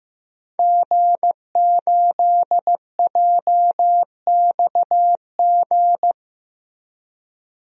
Morse code, heard as a keyed tone, about 15 wpm, 700 Hz. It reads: G8JXG
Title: DXCC-100 Call Signs - 5 Characters - Repeated 2x - 15wpm